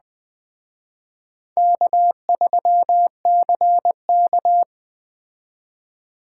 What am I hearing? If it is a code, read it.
Morse code, heard as a keyed tone, about 21 wpm, 700 Hz. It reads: K3CK
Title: Speed Racing - USA 1x2 Call Signs 17wpm